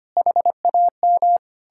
Morse code, heard as a keyed tone, about 33 wpm, 700 Hz. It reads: HAM